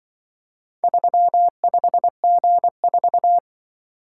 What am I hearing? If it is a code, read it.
Morse code, heard as a keyed tone, about 24 wpm, 700 Hz. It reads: 35G4
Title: Speed-Racing - Four Letter-Number Combinations - 20wpm